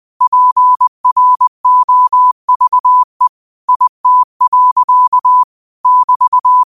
Morse code, heard as a keyed tone, about 20 wpm, 1000 Hz. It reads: PROVE IT. <BT>
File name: qotdfiles / 20wpm